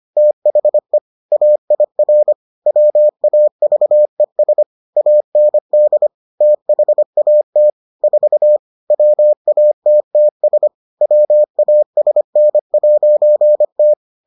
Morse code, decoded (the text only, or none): THE AIR WAVES AND THAT 4 WATTS WASN'T